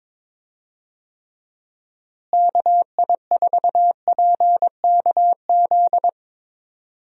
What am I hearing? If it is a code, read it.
Morse code, heard as a keyed tone, about 22 wpm, 700 Hz. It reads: KI4PKZ
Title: USA Call Signs - 6 Characters - Repeated 3x - 22wpm